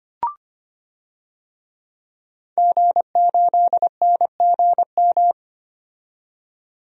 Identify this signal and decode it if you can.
Morse code, heard as a keyed tone, about 25 wpm, 690 Hz. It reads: G8NGM
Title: DXCC-100 Call Signs - 5 Characters - 25wpm